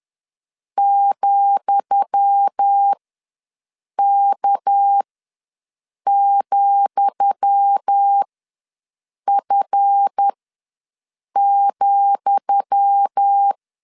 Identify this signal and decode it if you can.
Morse code, heard as a keyed tone, about 10 wpm, spaced as for 6 wpm, 800 Hz. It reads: ,K,F,
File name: LEZIONI / 10 SIGNS / 10D COMA ws 10wpm 4314